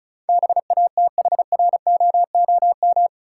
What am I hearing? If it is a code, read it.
Morse code, heard as a keyed tone, about 36 wpm, 700 Hz. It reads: BATHROOM
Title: Sets of 5 Words Encore - New Word Drill - 35wpm